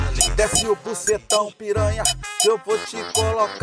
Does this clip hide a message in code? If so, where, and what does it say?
No.